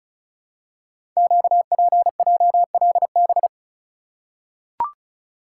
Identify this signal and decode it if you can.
Morse code, heard as a keyed tone, about 35 wpm, 690 Hz. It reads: QPJLB
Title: Code Groups - 5 Letters - 35wpm